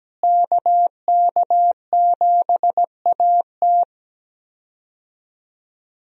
Morse code, heard as a keyed tone, about 17 wpm, 700 Hz. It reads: KK7AT
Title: USA Call Signs - 5 Characters - Repeated 3x - 17wpm